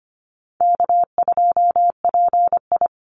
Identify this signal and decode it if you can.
Morse code, heard as a keyed tone, about 25 wpm, 700 Hz. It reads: K2PI
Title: Easy Call Signs Repeated 3x - 25wpm